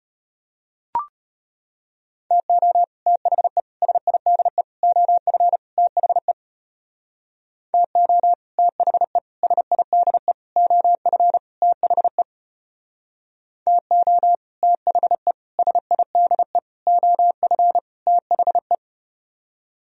Morse code, heard as a keyed tone, about 33 wpm, 700 Hz. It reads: TO THE SIDE OF THE TO THE SIDE OF THE TO THE SIDE OF THE